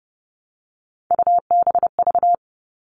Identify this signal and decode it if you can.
Morse code, heard as a keyed tone, about 30 wpm, 700 Hz. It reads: UBV
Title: All Cut Numbers - 3-Digit 30wpm